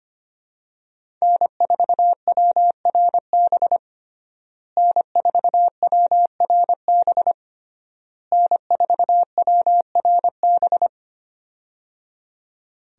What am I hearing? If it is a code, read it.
Morse code, heard as a keyed tone, about 25 wpm, 700 Hz. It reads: N4WRB N4WRB N4WRB
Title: USA Call Signs - 5 Characters - Repeated 3x - 25wpm